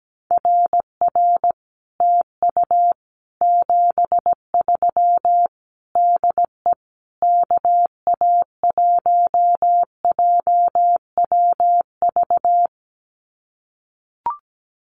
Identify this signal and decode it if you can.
Morse code, heard as a keyed tone, about 17 wpm, 700 Hz. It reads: RR TU 73 DE KA1JWV
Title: Common QSO Phrases Encore 17wpm